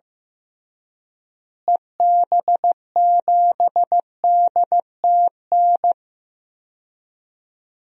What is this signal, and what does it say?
Morse code, heard as a keyed tone, about 15 wpm, 700 Hz. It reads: EB7DTN
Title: DXCC-100 Call Signs - 6 Characters - Repeated 2x - 15wpm